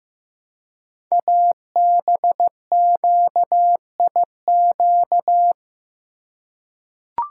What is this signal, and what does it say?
Morse code, heard as a keyed tone, about 15 wpm, 700 Hz. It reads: ABQIQ
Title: Code Groups - 5 Letters - 15wpm